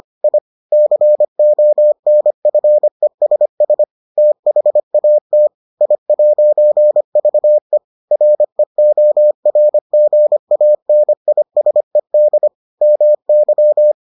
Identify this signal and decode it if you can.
Morse code, heard as a keyed tone, about 25 wpm, 600 Hz. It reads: I CONFESS THAT I'VE REORGANISED MY